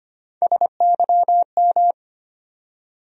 Morse code, heard as a keyed tone, about 25 wpm, 700 Hz. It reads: SYM
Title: USA Call Sign Postfixes - 3 Letters 25wpm